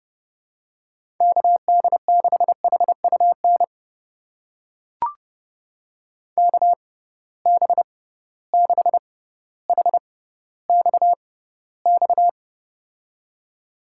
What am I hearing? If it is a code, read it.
Morse code, heard as a keyed tone, about 30 wpm, spaced as for 15 wpm, 700 Hz. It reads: KD6HUN KB6HXX
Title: USA Call Signs - 6 Characters - 6x Character Spacing - 30wpm